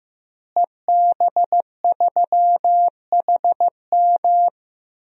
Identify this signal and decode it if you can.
Morse code, heard as a keyed tone, about 15 wpm, 700 Hz. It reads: EB3HM
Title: DXCC-100 Call Signs - 5 Characters - Repeated 2x - 15wpm